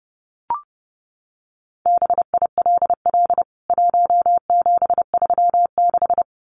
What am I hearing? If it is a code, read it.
Morse code, heard as a keyed tone, about 30 wpm, 700 Hz. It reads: BILL 1736
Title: CWT Exchange 30wpm